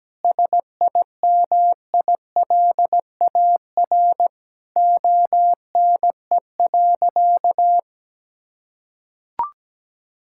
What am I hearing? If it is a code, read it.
Morse code, heard as a keyed tone, about 17 wpm, 700 Hz. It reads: SIMILAR ONE.
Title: Sentences from Top 1000 Words - Review Part 5 - 17wpm